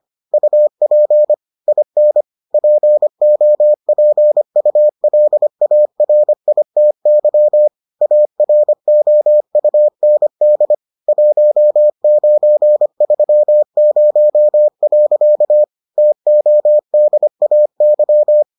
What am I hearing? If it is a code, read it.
Morse code, heard as a keyed tone, about 25 wpm, 600 Hz. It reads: UP IN POPULARITY AROUND 1930. TODAY